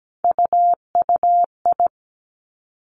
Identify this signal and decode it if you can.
Morse code, heard as a keyed tone, about 17 wpm, 700 Hz. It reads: UUI
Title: Three Challenging Letter-Number Combinations 17wpm